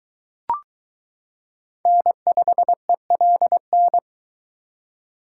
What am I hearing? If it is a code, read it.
Morse code, heard as a keyed tone, about 22 wpm, 700 Hz. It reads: N5ELN